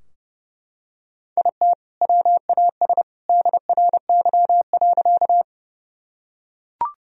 Morse code, heard as a keyed tone, about 30 wpm, 700 Hz. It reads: IT WAS DRY.
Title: Sentences from Top 1000 Words - Review Part 3 - 30wpm